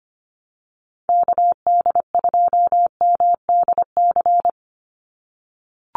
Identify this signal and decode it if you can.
Morse code, heard as a keyed tone, about 25 wpm, 700 Hz. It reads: KD2MDC
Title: USA Call Signs 25wpm